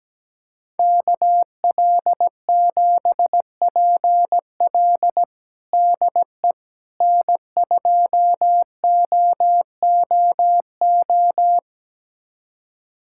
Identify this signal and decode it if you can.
Morse code, heard as a keyed tone, about 17 wpm, 700 Hz. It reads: KL7PL DE N2OOO
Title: USA Callsign DE Callsign - 5 Characters - 17wpm